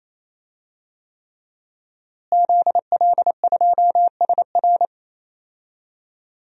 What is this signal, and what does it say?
Morse code, heard as a keyed tone, about 28 wpm, 700 Hz. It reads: ZL2SR
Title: DXCC-100 Call Signs - 5 Characters - Repeated 3x - 28wpm